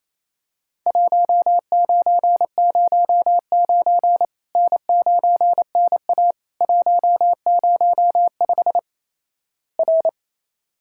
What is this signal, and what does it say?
Morse code, heard as a keyed tone, about 28 wpm, 700 Hz. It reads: 1909 N9NA 105 R
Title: Worked All Europe DX Contest - QTC Exchange - 28wpm